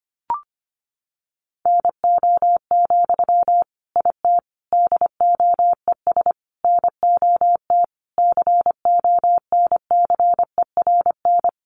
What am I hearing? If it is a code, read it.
Morse code, heard as a keyed tone, about 25 wpm, 700 Hz. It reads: NO, IT DOES NOT CONCERN
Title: Sentences from Top 1000 Words 25wpm